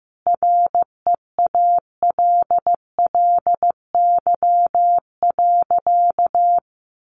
Morse code, heard as a keyed tone, about 15 wpm, 700 Hz. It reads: REALLY.
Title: Sentences from Top 600 Words 15wpm